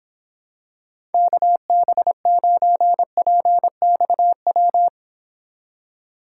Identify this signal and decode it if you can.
Morse code, heard as a keyed tone, about 25 wpm, 700 Hz. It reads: KB9PXW